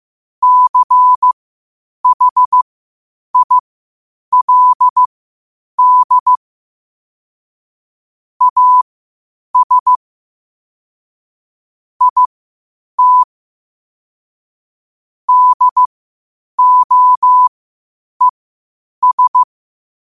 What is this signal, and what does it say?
Morse code, heard as a keyed tone, about 15 wpm, spaced as for 8 wpm, 1000 Hz. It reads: CHILD AS IT DOES
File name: qotdfiles / 8wpm